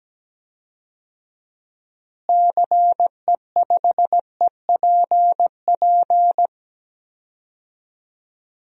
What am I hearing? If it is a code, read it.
Morse code, heard as a keyed tone, about 17 wpm, 700 Hz. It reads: CE5EPP
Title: DXCC-100 Call Signs - 6 Characters - Repeated 3x - 17wpm